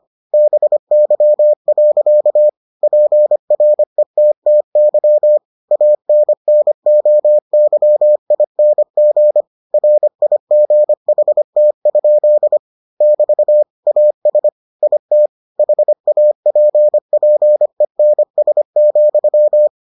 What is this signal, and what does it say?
Morse code, heard as a keyed tone, about 25 wpm, 600 Hz. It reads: BY. PRETTY ANNOYING RIGHT? <BT> AS IT HAPPENS,